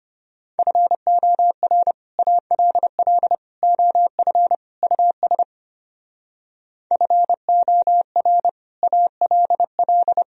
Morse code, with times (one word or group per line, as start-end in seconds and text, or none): 0.6-1.9 FOR
2.2-3.4 ALL
3.6-4.5 OF
4.8-5.4 US
6.9-8.5 FOR
8.8-10.2 ALL